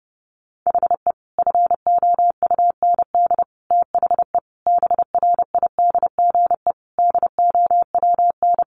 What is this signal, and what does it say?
Morse code, heard as a keyed tone, about 30 wpm, 700 Hz. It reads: HE FOUND THE BRIDGE DOWN
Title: Sentences from Top 1200 Words 30wpm